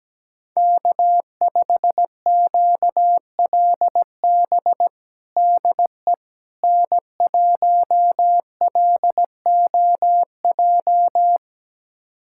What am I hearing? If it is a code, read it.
Morse code, heard as a keyed tone, about 17 wpm, 700 Hz. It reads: K5QLB DE N1LOJ
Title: USA Callsign DE Callsign - 5 Characters - 17wpm